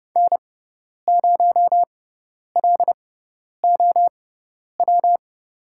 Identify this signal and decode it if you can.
Morse code, heard as a keyed tone, about 30 wpm, spaced as for 10 wpm, 700 Hz. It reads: N0LOW
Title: USA Call Signs - 5 Characters - 6x Character Spacing - 30wpm